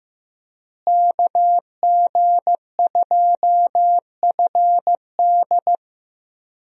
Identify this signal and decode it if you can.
Morse code, heard as a keyed tone, about 15 wpm, 700 Hz. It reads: KG2FD
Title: USA Call Signs - 5 Characters - Repeated 2x - 15wpm